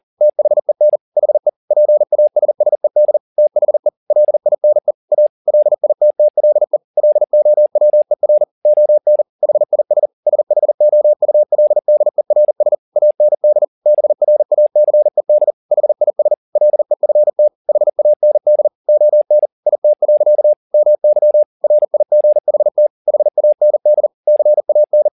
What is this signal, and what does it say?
Morse code, heard as a keyed tone, about 40 wpm, 600 Hz. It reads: THEN HE PASSED THE LINE A LITTLE LOWER ON HIS SHOULDERS AND BRACED HIS LEFT HAND ON IT. MY RIGHT HAND CAN
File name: the old man and the sea 2 40wpm